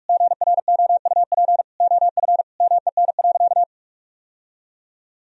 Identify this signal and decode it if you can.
Morse code, heard as a keyed tone, about 46 wpm, 700 Hz. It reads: GROUP OF MEN.
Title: Sentences from Top 1000 Words - Review Part 1 - 45wpm